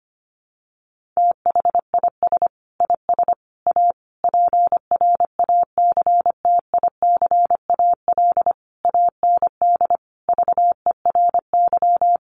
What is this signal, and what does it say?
Morse code, heard as a keyed tone, about 25 wpm, 700 Hz. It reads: THIS IS A PRACTICAL AND VERY
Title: Sentences from Top 1400 Words 25wpm